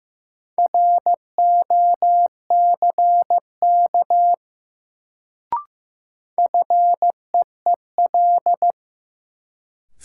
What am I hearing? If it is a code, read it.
Morse code, heard as a keyed tone, about 15 wpm, 700 Hz. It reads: ROCK FEEL
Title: Sentences from Top 400 Words - New Word Drill - 15wpm